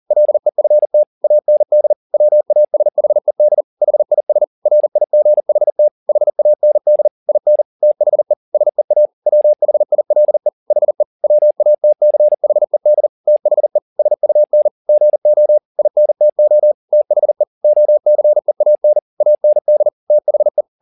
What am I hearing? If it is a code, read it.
Morse code, heard as a keyed tone, about 40 wpm, 600 Hz. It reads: LEFT AND WASHED HIS RIGHT HAND IN THE SEA WHILE HE WATCHED THE SUN GO INTO THE OCEAN AND THE
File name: the old man and the sea 2 40wpm